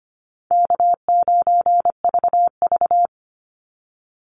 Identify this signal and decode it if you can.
Morse code, heard as a keyed tone, about 25 wpm, 700 Hz. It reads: K9VV
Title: Call Signs Repeated 3x - 25wpm